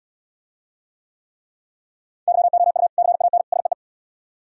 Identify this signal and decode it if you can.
Morse code, heard as a keyed tone, about 77 wpm, 700 Hz. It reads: YOU CAN SEE